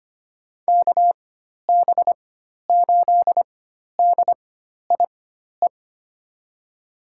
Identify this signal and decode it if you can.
Morse code, heard as a keyed tone, about 25 wpm, spaced as for 12 wpm, 700 Hz. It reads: KB8DIE
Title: USA Call Signs - 6 Characters - 4x Character Spacing - 25wpm